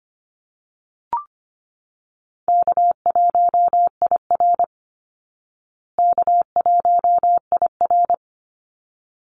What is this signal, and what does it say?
Morse code, heard as a keyed tone, about 25 wpm, 700 Hz. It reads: K1IR K1IR
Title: Easy Call Signs Repeated 2x - 25wpm